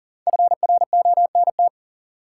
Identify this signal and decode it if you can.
Morse code, heard as a keyed tone, about 40 wpm, 700 Hz. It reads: FRONT
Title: Sets of 2 Words - New Word Drill - 40wpm